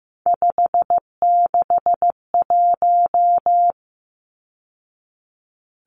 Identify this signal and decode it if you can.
Morse code, heard as a keyed tone, about 15 wpm, 700 Hz. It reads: 561
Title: Numbers - 3-Digit 15wpm